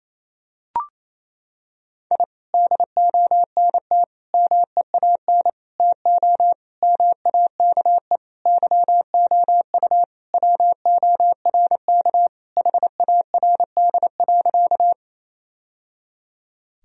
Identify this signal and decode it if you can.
Morse code, heard as a keyed tone, about 28 wpm, 700 Hz. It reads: I DONT MEAN TO MAKE YOU WORK HARD.
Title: Sentences from Top 1000 Words - Review Part 1 - 28wpm